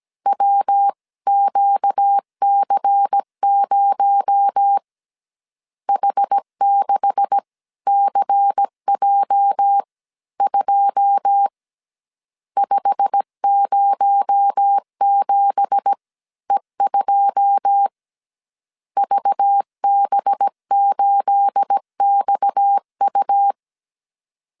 Morse code, heard as a keyed tone, about 17 wpm, spaced as for 15 wpm, 800 Hz. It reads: WQC0 H6CJ2 507E2 VB8XU
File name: CW / DET / 1 9 1500C S C 16wpm 2909